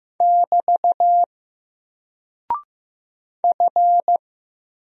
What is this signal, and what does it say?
Morse code, heard as a keyed tone, about 15 wpm, 700 Hz. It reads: <BT> F
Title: Single Character (Letter:Number:Punctuation:Prosign) 15wpm